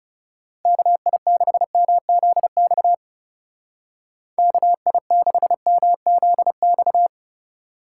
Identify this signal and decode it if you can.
Morse code, heard as a keyed tone, about 37 wpm, 700 Hz. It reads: KI6MZX KI6MZX
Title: Speed Racing - USA 2x3 Call Signs 30wpm